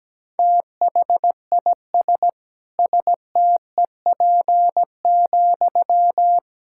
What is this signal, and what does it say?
Morse code, heard as a keyed tone, about 17 wpm, 700 Hz. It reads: THIS STEP,
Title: Sentences from Top 1000 Words - Review Part 6 - 17wpm